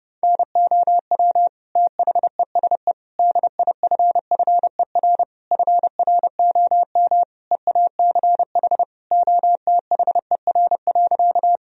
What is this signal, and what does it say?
Morse code, heard as a keyed tone, about 30 wpm, 700 Hz. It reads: NOW THESE DIFFER FROM EACH OTHER.
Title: Sentences from Top 1000 Words - Review Part 1 - 30wpm